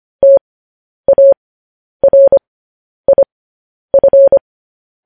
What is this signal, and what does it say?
Morse code, heard as a keyed tone, about 25 wpm, spaced as for 10 wpm, 550 Hz. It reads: TARIF
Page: news-headlines-in-morse-code-at-10-wpm